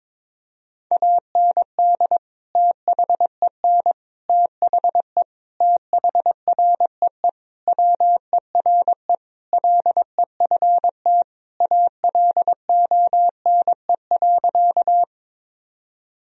Morse code, heard as a keyed tone, about 22 wpm, 700 Hz. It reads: AND THEN THE THREE WERE LEFT ALONE.
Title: Sentences from Top 1000 Words - Review Part 1 - 22wpm